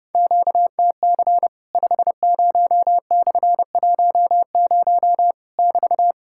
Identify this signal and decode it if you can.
Morse code, heard as a keyed tone, about 30 wpm, 700 Hz. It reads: QTC 50/10 <BT>
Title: Worked All Europe DX Contest - QTC Exchange - 30wpm